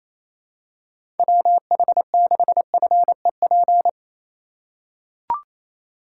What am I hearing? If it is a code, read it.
Morse code, heard as a keyed tone, about 28 wpm, 700 Hz. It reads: WH6FEP
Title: USA Call Signs - 6 Characters - 4x Character Spacing - 28wpm